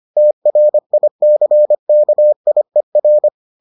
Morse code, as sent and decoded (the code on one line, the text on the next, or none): - .-. .. -.-. -.- .. .-.
TRICKIER